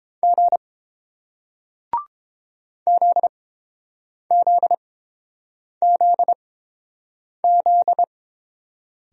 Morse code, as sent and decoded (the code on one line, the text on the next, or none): --. / --.. --.. --.. --..
G ZZZZ